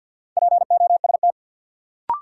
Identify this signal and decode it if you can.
Morse code, heard as a keyed tone, about 53 wpm, 700 Hz. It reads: POST
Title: Sentences from Top 1000 Words - New Word Drill - 50wpm